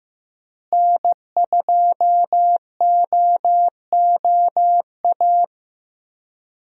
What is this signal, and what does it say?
Morse code, heard as a keyed tone, about 15 wpm, 700 Hz. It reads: N2OOA
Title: USA Call Signs - 5 Characters - Repeated 2x - 15wpm